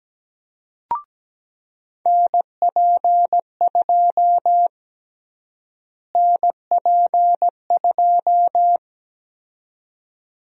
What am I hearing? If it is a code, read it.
Morse code, heard as a keyed tone, about 17 wpm, 700 Hz. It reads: NP2 NP2
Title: USA Call Sign Prefixes - Repeated 2x - 17wpm